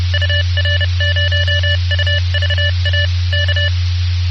Morse code, heard as a keyed tone, about 30 wpm, 1750 Hz, with noise. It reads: UR0UVA K